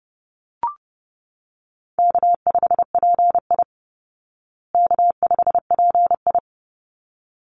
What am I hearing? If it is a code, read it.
Morse code, heard as a keyed tone, about 30 wpm, 700 Hz. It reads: K5PI K5PI
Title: Easy Call Signs Repeated 2x - 30wpm